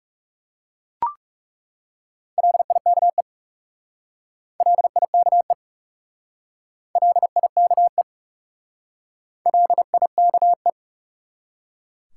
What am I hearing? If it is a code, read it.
Morse code, heard as a keyed tone, about 39 wpm, 700 Hz. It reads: LIKE LIKE LIKE LIKE